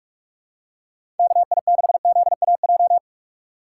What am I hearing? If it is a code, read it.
Morse code, heard as a keyed tone, about 46 wpm, 700 Hz. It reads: KI6ZAJ
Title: USA Call Signs - 6 Characters - 3x Character Spacing - 45wpm